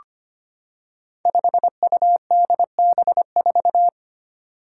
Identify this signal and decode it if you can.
Morse code, heard as a keyed tone, about 25 wpm, 700 Hz. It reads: 5UDB4
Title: Five Challenging Letter-Number Combinations 25wpm